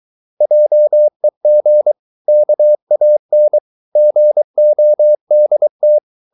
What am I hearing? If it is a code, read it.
Morse code, heard as a keyed tone, about 23 wpm, 600 Hz. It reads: JEG KAN GODT
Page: Ferie 23 wpm